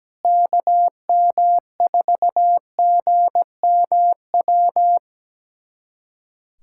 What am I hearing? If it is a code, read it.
Morse code, heard as a keyed tone, about 17 wpm, 700 Hz. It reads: KM4GMW
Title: USA Callsign DE Callsign - 6 Characters - 17wpm